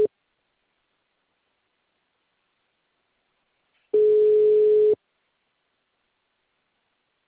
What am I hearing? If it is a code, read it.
Ringback tone (Europe, 425 Hz).
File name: out- -2012- - - .34182